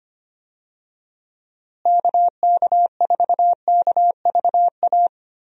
Morse code, heard as a keyed tone, about 25 wpm, 700 Hz. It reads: KK4KVA
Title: USA Call Signs - 6 Characters - Repeated 2x - 25wpm